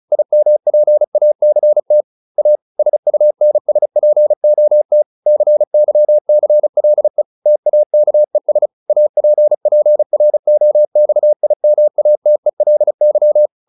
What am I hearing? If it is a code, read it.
Morse code, heard as a keyed tone, about 34 wpm, 600 Hz. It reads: IMPACT A SUNSPOT CYCLE TAKES APPROXIMATELY